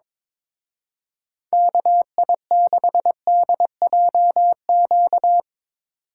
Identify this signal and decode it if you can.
Morse code, heard as a keyed tone, about 22 wpm, 700 Hz. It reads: KI6DJQ
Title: Speed Racing - USA 2x3 Call Signs 22wpm